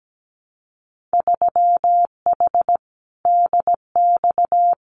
Morse code, heard as a keyed tone, about 17 wpm, 700 Hz. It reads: 3H DX
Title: Winter Field Day Exchanges 17wpm